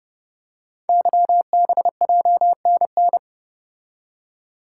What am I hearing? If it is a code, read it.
Morse code, heard as a keyed tone, about 29 wpm, 700 Hz. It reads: YBJNN
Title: Code Groups - 5 Letters - 30wpm